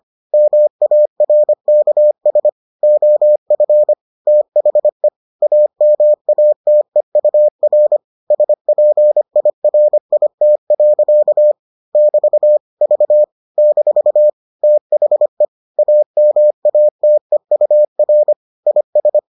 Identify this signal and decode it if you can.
Morse code, heard as a keyed tone, about 25 wpm, 600 Hz. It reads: MARKS OF THE AMATEUR SPIRIT. <BT> V - THE AMATEUR IS